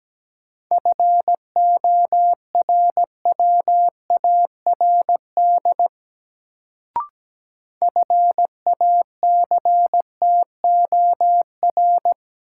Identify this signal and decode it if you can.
Morse code, heard as a keyed tone, about 17 wpm, 700 Hz. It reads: FORWARD FACTOR
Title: Sentences from Top 1300 Words - New Word Drill Alt - 17wpm